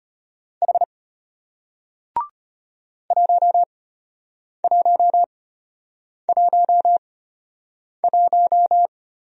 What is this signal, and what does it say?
Morse code, heard as a keyed tone, about 33 wpm, spaced as for 8 wpm, 700 Hz. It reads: H 1111